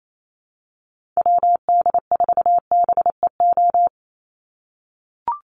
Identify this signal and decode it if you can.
Morse code, heard as a keyed tone, about 28 wpm, 700 Hz. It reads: WD4BEO
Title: USA Call Signs 28wpm